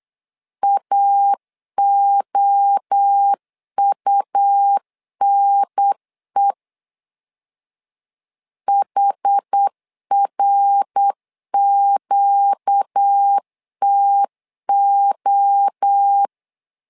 Morse code, heard as a keyed tone, about 8 wpm, 800 Hz. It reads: AOUNE HRQTO
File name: 4 Q 8wpm